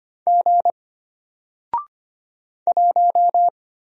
Morse code, heard as a keyed tone, about 25 wpm, 700 Hz. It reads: G 1
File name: Single-Letter-Number-25wpm